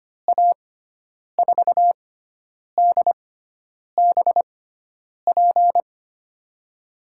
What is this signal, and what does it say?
Morse code, heard as a keyed tone, about 25 wpm, spaced as for 9 wpm, 700 Hz. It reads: A4DBP
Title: USA Call Signs - 6 Characters - 6x Character Spacing - 25wpm